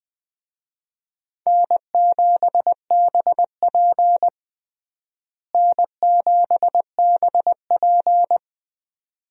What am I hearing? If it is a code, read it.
Morse code, heard as a keyed tone, about 20 wpm, 700 Hz. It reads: N7BP N7BP
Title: USA Call Signs - 4 Characters - Repeated 2x - 20wpm